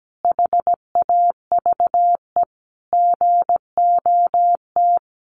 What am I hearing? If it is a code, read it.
Morse code, heard as a keyed tone, about 17 wpm, 700 Hz. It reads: HAVE GOT
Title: Sentences from Top 1500 Words 17wpm